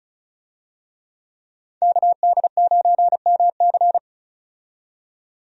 Morse code, heard as a keyed tone, about 36 wpm, 700 Hz. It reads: KD9MC
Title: USA Call Signs - 5 Characters - 35wpm